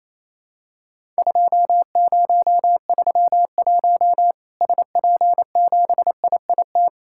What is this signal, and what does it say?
Morse code, heard as a keyed tone, about 27 wpm, 700 Hz. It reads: 2031 SP7IIT
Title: Worked All Europe DX Contest - QTC Exchange - 28wpm